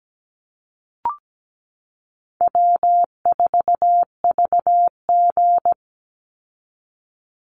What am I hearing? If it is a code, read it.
Morse code, heard as a keyed tone, about 17 wpm, 700 Hz. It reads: W4VG
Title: Easy Call Signs - 17wpm